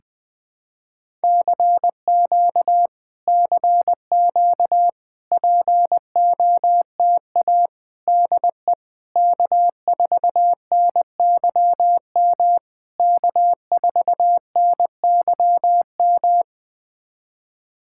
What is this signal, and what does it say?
Morse code, heard as a keyed tone, about 20 wpm, 660 Hz. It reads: CQ CQ POTA DE K4NYM K4NYM
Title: POTA QSO Practice - 20wpm